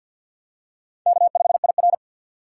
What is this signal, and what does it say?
Morse code, heard as a keyed tone, about 52 wpm, 700 Hz. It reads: K5IR
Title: USA Call Signs - 4 Characters - 2x Character Spacing - 50wpm